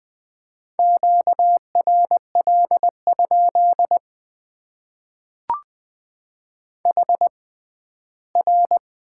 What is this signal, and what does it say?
Morse code, heard as a keyed tone, about 20 wpm, spaced as for 14 wpm, 700 Hz. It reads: QRL? HR
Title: Top 90 QSO Elements - 6x Character Spacing - 20wpm